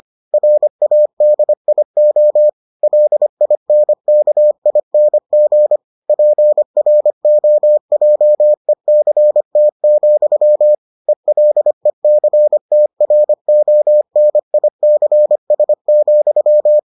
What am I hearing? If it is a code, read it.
Morse code, heard as a keyed tone, about 25 wpm, 600 Hz. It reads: RADIO LINKING PROJECT, ELECTRONICS,